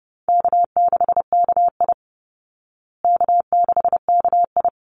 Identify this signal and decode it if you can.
Morse code, heard as a keyed tone, about 29 wpm, 700 Hz. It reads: K6KI K6KI
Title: Easy Call Signs Repeated 2x - 30wpm